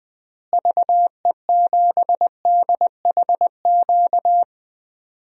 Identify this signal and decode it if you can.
Morse code, heard as a keyed tone, about 20 wpm, 700 Hz. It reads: VE7DHQ
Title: DXCC-100 Call Signs - 6 Characters - Repeated 3x - 20wpm